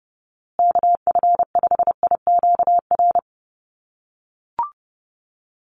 Morse code, heard as a keyed tone, about 30 wpm, 700 Hz. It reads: KF5IQR
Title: USA Call Signs Repeated 3x 30wpm